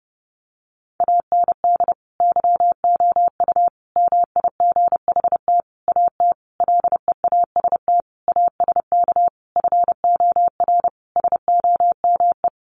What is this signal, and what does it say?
Morse code, heard as a keyed tone, about 30 wpm, 700 Hz. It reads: AND YOU MIGHT AT LEAST ASK FOR SOME
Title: Sentences from Top 1500 Words 30wpm